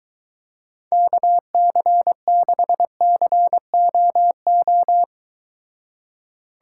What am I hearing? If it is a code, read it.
Morse code, heard as a keyed tone, about 23 wpm, 700 Hz. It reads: KC6COO